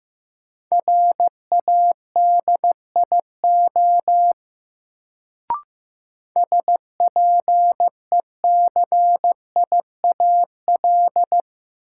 Morse code, heard as a keyed tone, about 15 wpm, 700 Hz. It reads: RADIO SPECIAL
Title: Sentences from Top 1000 Words - Review Part 3 - New Word Drill - 15wpm